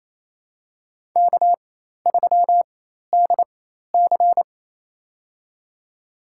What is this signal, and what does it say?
Morse code, heard as a keyed tone, about 28 wpm, spaced as for 13 wpm, 700 Hz. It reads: K3DC
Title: USA Call Signs - 4x Character Spacing - 28wpm